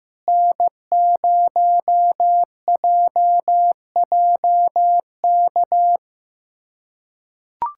Morse code, heard as a keyed tone, about 15 wpm, 700 Hz. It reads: N0JJK
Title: USA Callsign DE Callsign - 5 Characters - 15wpm